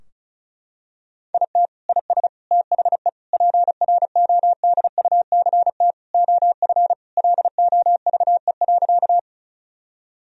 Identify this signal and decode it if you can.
Morse code, heard as a keyed tone, about 36 wpm, 700 Hz. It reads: IT IS THE PRODUCT OF LOVE.
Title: Sentences from Top 1000 Words - Review Part 2 - 35wpm